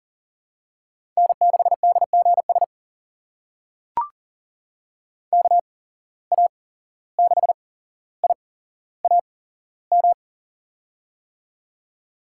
Morse code, heard as a keyed tone, about 38 wpm, spaced as for 16 wpm, 690 Hz. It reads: N6DGS KA6IAM
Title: USA Call Signs - 8x Character Spacing - 40wpm